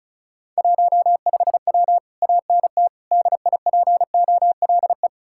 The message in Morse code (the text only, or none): 15W ANT DIPOLE